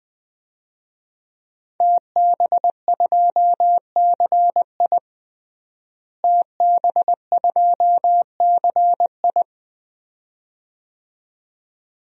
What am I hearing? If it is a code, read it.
Morse code, heard as a keyed tone, about 21 wpm, 700 Hz. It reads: TB2CI TB2CI
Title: DXCC-100 Call Signs - 5 Characters - Repeated 2x - 20wpm